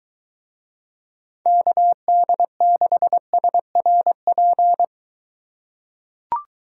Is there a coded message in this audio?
Yes: Morse code from 1.5 to 4.8 s, heard as a keyed tone, about 23 wpm, 700 Hz. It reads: KD6SRP